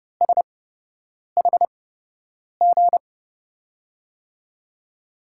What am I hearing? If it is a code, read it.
Morse code, heard as a keyed tone, about 30 wpm, spaced as for 8 wpm, 700 Hz. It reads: SHG
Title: USA Call Signs - 5 Characters - 8x Character Spacing - 30wpm